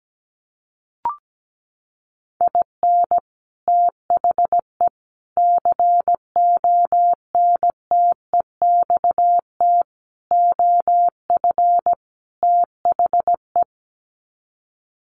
Morse code, heard as a keyed tone, about 17 wpm, 700 Hz. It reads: IN THE CONTEXT OF THE
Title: Sets of 5 Words 17wpm